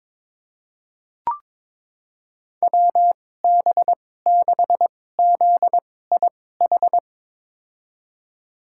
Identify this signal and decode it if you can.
Morse code, heard as a keyed tone, about 22 wpm, spaced as for 16 wpm, 700 Hz. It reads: WB6ZIH
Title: USA Call Signs - 6 Characters - 2x Character Spacing - 22wpm